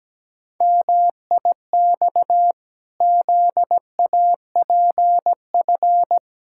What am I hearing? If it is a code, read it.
Morse code, heard as a keyed tone, about 17 wpm, 700 Hz. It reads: MIX ZAPF
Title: Pangram Challenge v2 - 17wpm